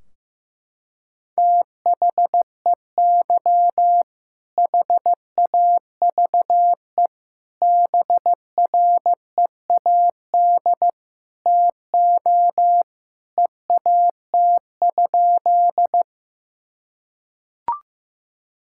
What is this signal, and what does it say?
Morse code, heard as a keyed tone, about 15 wpm, 700 Hz. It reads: THEY HAVE BREAD TO EAT?
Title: Sentences from Top 1600 Words 15wpm